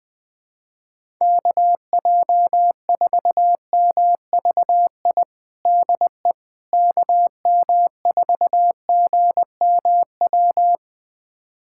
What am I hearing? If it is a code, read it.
Morse code, heard as a keyed tone, about 20 wpm, 700 Hz. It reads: KJ4MVI DE KM4GMW
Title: USA Callsign DE Callsign - 6 Characters - 20wpm